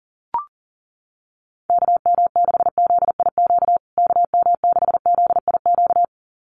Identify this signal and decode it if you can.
Morse code, heard as a keyed tone, about 40 wpm, 700 Hz. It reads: KM6ZIQ KM6ZIQ
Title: USA Call Signs Repeated 2x 40wpm